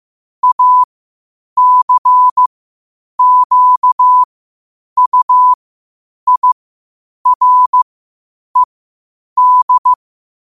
Morse code, heard as a keyed tone, about 15 wpm, spaced as for 8 wpm, 1000 Hz. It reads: ACQUIRED
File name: qotdfiles / 8wpm